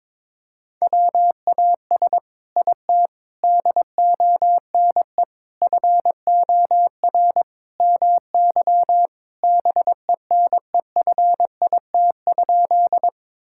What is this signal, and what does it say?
Morse code, heard as a keyed tone, about 22 wpm, 700 Hz. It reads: WAS IT DONE FOR MY BENEFIT?
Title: Sentences from Top 1000 Words - Review Part 6 - 22wpm